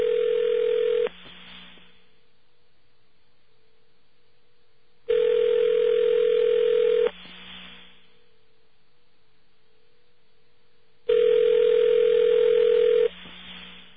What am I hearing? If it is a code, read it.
Ringback tone (North America), 440 + 480 Hz.